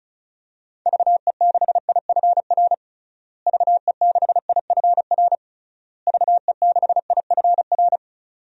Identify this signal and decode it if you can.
Morse code, heard as a keyed tone, about 36 wpm, 700 Hz. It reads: VE6IFR VE6IFR VE6IFR
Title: DXCC-100 Call Signs - 6 Characters - Repeated 3x - 35wpm